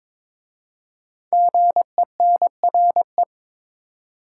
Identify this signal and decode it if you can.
Morse code, heard as a keyed tone, about 22 wpm, 700 Hz. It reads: GENRE
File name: 5 Letter Words - First 3 Letters Given - 22wpm